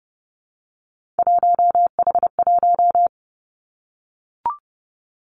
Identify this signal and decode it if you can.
Morse code, heard as a keyed tone, about 30 wpm, 700 Hz. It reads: 1H1
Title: Three Challenging Letter-Number Combinations 30wpm